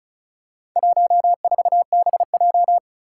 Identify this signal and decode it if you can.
Morse code, heard as a keyed tone, about 36 wpm, 690 Hz. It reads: 14BJ
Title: Four Challenging Letter-Number Combinations 35wpm